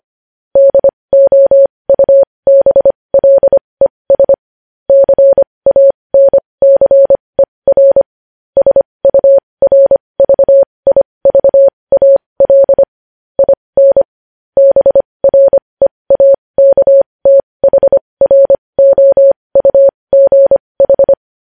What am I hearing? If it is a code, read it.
Morse code, heard as a keyed tone, about 25 wpm, spaced as for 20 wpm, 550 Hz. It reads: DOUBLES CANCER SURVIVAL IN BREAKTHROUGH
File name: podcast / News headlines-1x-25x20